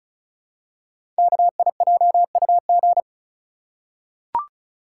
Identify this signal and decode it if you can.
Morse code, heard as a keyed tone, about 36 wpm, 690 Hz. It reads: KIJUG
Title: Code Groups - 5 Letters - 35wpm